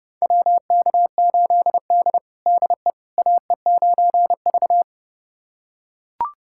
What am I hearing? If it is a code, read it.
Morse code, heard as a keyed tone, about 30 wpm, 700 Hz. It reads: WK8D DE AE9V
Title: USA Callsign DE Callsign - 4 Characters - 30wpm